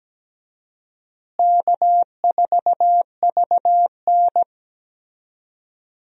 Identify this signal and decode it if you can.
Morse code, heard as a keyed tone, about 17 wpm, 700 Hz. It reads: K4VN
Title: USA Call Signs - 4 Characters - Repeated 2x - 17wpm